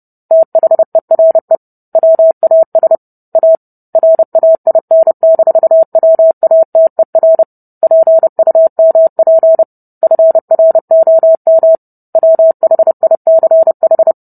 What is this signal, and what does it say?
Morse code, heard as a keyed tone, about 30 wpm, 650 Hz. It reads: THERE WAS A RAIN-WATER PUMP FROM WHICH